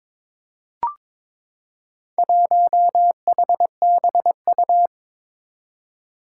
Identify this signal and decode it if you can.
Morse code, heard as a keyed tone, about 22 wpm, 700 Hz. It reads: 1HBU
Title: Four Challenging Letter-Number Combinations 22wpm